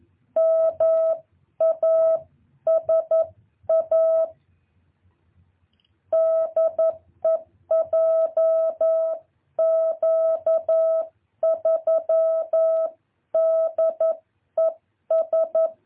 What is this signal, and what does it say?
Morse code, heard as a keyed tone, about 11 wpm, 650 Hz. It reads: MASA DEJQ3DES
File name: jh8cbh / 2023cw